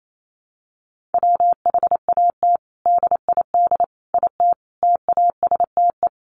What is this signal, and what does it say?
Morse code, heard as a keyed tone, about 28 wpm, 700 Hz. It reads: WHAT DID IT TASTE
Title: Sentences from Top 1100 Words 28wpm